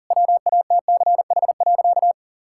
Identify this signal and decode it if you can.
Morse code, heard as a keyed tone, about 40 wpm, 700 Hz. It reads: WATCH.
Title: Sentences from Top 1600 Words 40wpm